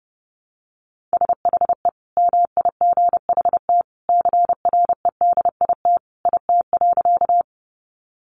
Morse code, heard as a keyed tone, about 30 wpm, 700 Hz. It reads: SHE MIGHT CREDIT IT.
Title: Sentences from Top 1000 Words 30wpm